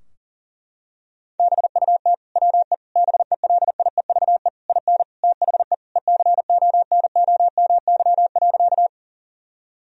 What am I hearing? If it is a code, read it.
Morse code, heard as a keyed tone, about 40 wpm, 700 Hz. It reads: BUT WE BELIEVE IN THE ECONOMY.
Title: Sentences from Top 1000 Words - Review Part 3 - 40wpm